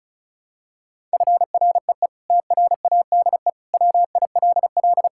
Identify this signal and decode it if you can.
Morse code, heard as a keyed tone, about 36 wpm, 700 Hz. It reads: FREE TRADE WILL
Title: Sentences from Top 1000 Words - Review Part 6 - 35wpm